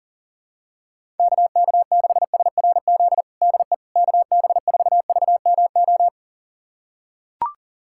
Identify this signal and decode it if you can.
Morse code, heard as a keyed tone, about 40 wpm, 700 Hz. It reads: KK6SRZ DE KB4VMO
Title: USA Callsign DE Callsign - 6 Characters - 40wpm